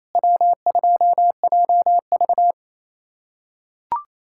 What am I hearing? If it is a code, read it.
Morse code, heard as a keyed tone, about 28 wpm, 700 Hz. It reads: W2JV
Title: USA Callsign DE Callsign - 4 Characters - 28wpm